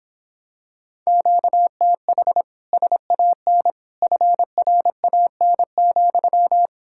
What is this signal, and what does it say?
Morse code, heard as a keyed tone, about 26 wpm, 700 Hz. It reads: QTH SAN FRAN,